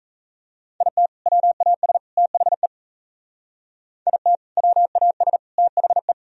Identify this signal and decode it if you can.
Morse code, heard as a keyed tone, about 36 wpm, 700 Hz. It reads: IT WAS THE IT WAS THE